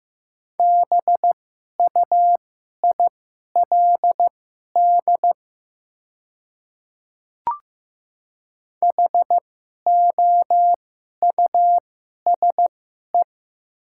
Morse code, heard as a keyed tone, about 15 wpm, spaced as for 11 wpm, 700 Hz. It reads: BUILD HOUSE
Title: 2x Character Spacing - Top 300 Words - 15wpm